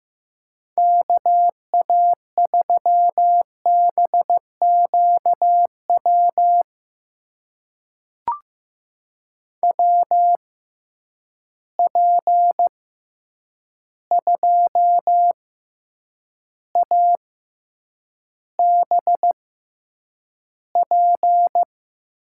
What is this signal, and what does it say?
Morse code, heard as a keyed tone, about 15 wpm, spaced as for 8 wpm, 700 Hz. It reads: KA3BQW WP2ABP
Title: USA Call Signs - 6 Characters - 6x Character Spacing - 15wpm